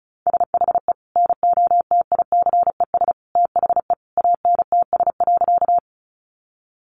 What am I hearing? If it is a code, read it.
Morse code, heard as a keyed tone, about 34 wpm, 700 Hz. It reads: SHE NOTICES THE ANTS.
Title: Additional Easy Phrases 35wpm